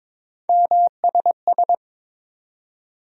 Morse code, heard as a keyed tone, about 22 wpm, 700 Hz. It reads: MSS
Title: USA Call Sign Postfixes - 3 Letters 22wpm